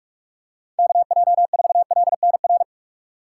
Morse code, heard as a keyed tone, about 41 wpm, 700 Hz. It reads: KJ4LNR